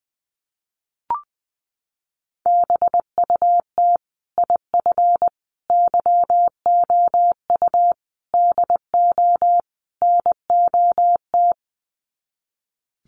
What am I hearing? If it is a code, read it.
Morse code, heard as a keyed tone, about 19 wpm, 700 Hz. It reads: BUT IF YOU DO NOT
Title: Sets of 5 Words Encore 20wpm